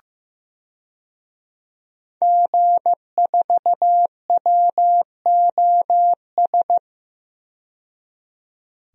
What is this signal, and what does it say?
Morse code, heard as a keyed tone, about 15 wpm, 700 Hz. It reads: G4WOS
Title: DXCC-100 Call Signs - 5 Characters - 15wpm